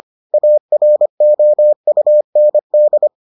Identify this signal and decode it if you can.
Morse code, heard as a keyed tone, about 25 wpm, 600 Hz. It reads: AROUND